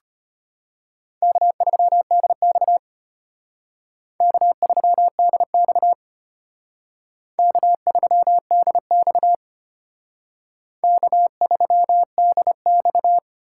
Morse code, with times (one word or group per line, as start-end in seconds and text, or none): 1.2-2.8 K3DX
4.2-5.9 K3DX
7.4-9.3 K3DX
10.8-13.2 K3DX